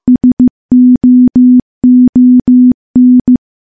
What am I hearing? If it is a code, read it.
Morse code, heard as a keyed tone, about 16 wpm, 250 Hz. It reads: SOON
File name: morse / morse-training-15WPM